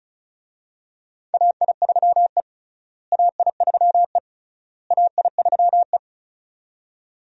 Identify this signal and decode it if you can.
Morse code, heard as a keyed tone, about 36 wpm, 700 Hz. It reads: AI3E AI3E AI3E
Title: USA Call Signs - 4 Characters - Repeated 3x - 35wpm